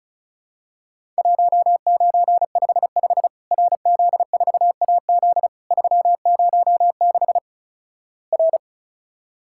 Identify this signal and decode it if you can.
Morse code, heard as a keyed tone, about 36 wpm, 700 Hz. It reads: 1955 RZ4AZ 306 R
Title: Worked All Europe DX Contest - QTC Exchange - 35wpm